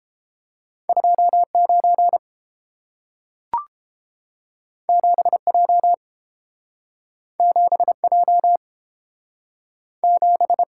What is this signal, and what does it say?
Morse code, heard as a keyed tone, about 30 wpm, 700 Hz. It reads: 29 7J 7J 7